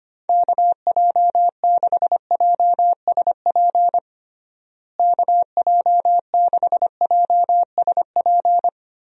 Morse code, heard as a keyed tone, about 25 wpm, 700 Hz. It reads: KJ6JSP KJ6JSP
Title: USA Call Signs - 6 Characters - Repeated 2x - 25wpm